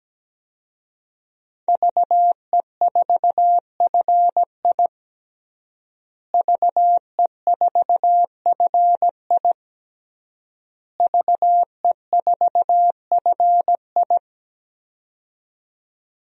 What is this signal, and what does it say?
Morse code, heard as a keyed tone, about 17 wpm, 700 Hz. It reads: VE4FI VE4FI VE4FI
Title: DXCC-100 Call Signs - 5 Characters - Repeated 3x - 17wpm